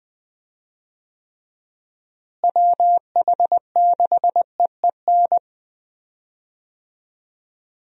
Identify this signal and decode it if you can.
Morse code, heard as a keyed tone, about 20 wpm, 700 Hz. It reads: WH6EEN
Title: USA Call Signs - 6 Characters - Repeated 2x - 20wpm